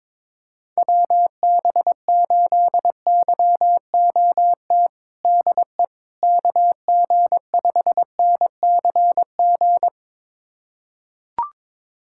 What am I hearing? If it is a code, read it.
Morse code, heard as a keyed tone, about 22 wpm, 700 Hz. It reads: WB8YOT DE KG5NCG
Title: USA Callsign DE Callsign 22wpm